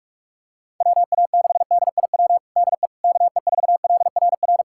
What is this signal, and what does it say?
Morse code, heard as a keyed tone, about 46 wpm, 700 Hz. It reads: WA6DIW DE KE4LRR
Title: USA Callsign DE Callsign 45wpm